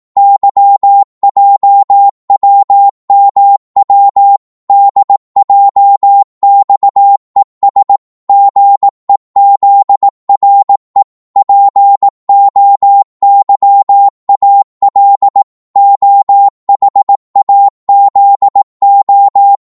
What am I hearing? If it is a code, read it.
Morse code, heard as a keyed tone, about 18 wpm, 800 Hz. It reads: YJWMW DJXES GEZRE POYAL OHAZO
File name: Morse / ltr-18wpm